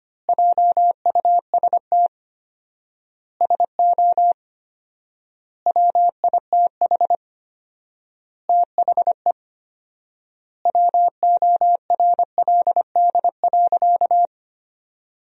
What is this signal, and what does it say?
Morse code, heard as a keyed tone, about 25 wpm, 700 Hz. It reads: JUST SO WITH THE WORLD.
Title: 4x Word-Spacing - Sentences from Top 200 Words - 25wpm